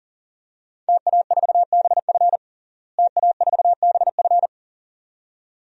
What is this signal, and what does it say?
Morse code, heard as a keyed tone, about 40 wpm, 700 Hz. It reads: TA4BF TA4BF
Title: DXCC-100 Call Signs - 5 Characters - Repeated 2x - 40wpm